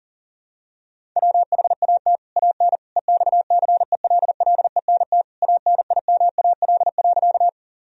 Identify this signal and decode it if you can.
Morse code, heard as a keyed tone, about 40 wpm, 700 Hz. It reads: WHAT AN EXCELLENT ANIMAL.
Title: Sentences from Top 1000 Words - Review Part 1 - 40wpm